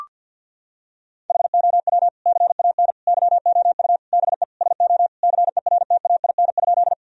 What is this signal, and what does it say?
Morse code, heard as a keyed tone, about 51 wpm, 700 Hz. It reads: HOW CAN YOU BE SO CERTAIN?
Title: Sentences from Top 1000 Words - Review Part 2 - 50wpm